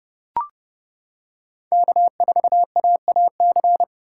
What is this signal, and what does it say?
Morse code, heard as a keyed tone, about 30 wpm, 700 Hz. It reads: K4AAC
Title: K1USN SST QSO Practice - 30wpm